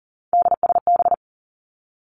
Morse code, heard as a keyed tone, about 42 wpm, 700 Hz. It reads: DSB
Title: Three Challenging Letter-Number Combinations 40wpm